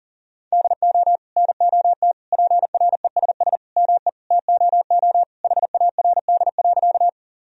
Morse code, heard as a keyed tone, about 40 wpm, 700 Hz. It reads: DO NOT PRESS ME TOO HARD.
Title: Sentences from Top 1000 Words - Review Part 1 - 40wpm